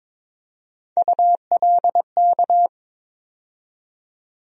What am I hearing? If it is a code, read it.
Morse code, heard as a keyed tone, about 22 wpm, 700 Hz. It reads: ULK
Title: USA Call Sign Postfixes - 3 Letters 22wpm